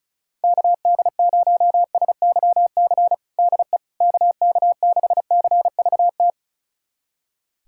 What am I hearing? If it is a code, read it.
Morse code, heard as a keyed tone, about 34 wpm, 700 Hz. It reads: KD0SYC DE KK6CVT
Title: USA Callsign DE Callsign 35wpm